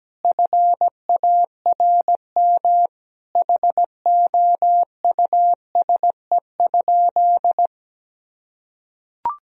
Morse code, heard as a keyed tone, about 17 wpm, 700 Hz. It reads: FARM HOUSE?
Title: Sentences from Top 1000 Words - Review Part 2 - 17wpm